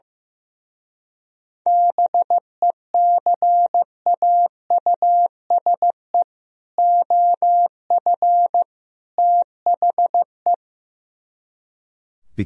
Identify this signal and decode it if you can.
Morse code, heard as a keyed tone, about 15 wpm, 700 Hz. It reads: BECAUSE OF THE
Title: Speed-Racing - Sets of 3 Words 15wpm